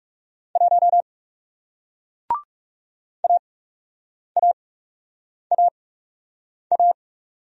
Morse code, heard as a keyed tone, about 41 wpm, spaced as for 8 wpm, 700 Hz. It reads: 1 AAAA